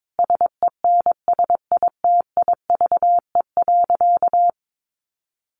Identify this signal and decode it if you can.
Morse code, heard as a keyed tone, about 22 wpm, 700 Hz. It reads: SENSITIVE.
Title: Sentences from Top 1400 Words 22wpm